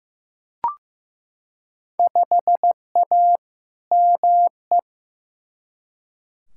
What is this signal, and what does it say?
Morse code, heard as a keyed tone, about 15 wpm, 700 Hz. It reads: 5A ME
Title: Field Day Exchanges Encore - 15wpm